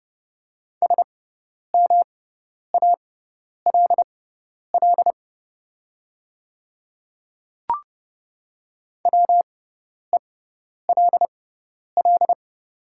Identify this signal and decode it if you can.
Morse code, heard as a keyed tone, about 30 wpm, spaced as for 10 wpm, 700 Hz. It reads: SMALL WELL
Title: 6x Character Spacing - Top 300 Words - 30wpm